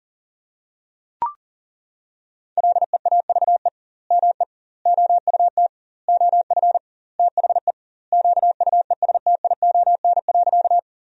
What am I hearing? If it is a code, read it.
Morse code, heard as a keyed tone, about 40 wpm, 700 Hz. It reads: LEAVE ME OUT OF THE QUESTION.
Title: 2x Word-Spacing - Sentences from Top 400 Words - 40wpm